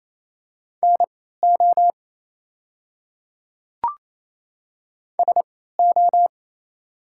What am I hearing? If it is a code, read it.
Morse code, heard as a keyed tone, about 28 wpm, spaced as for 16 wpm, 690 Hz. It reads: NO SO
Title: 3x Character Spacing - Top 100 Words - 28wpm